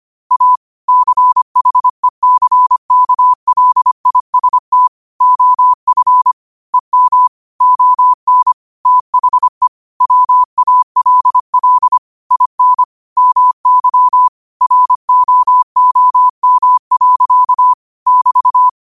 Morse code, heard as a keyed tone, about 25 wpm, 1000 Hz. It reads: A CHECKLIST OF EM ON THE WALL IN MY ROOM. <BT>
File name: qotdfiles / 25wpm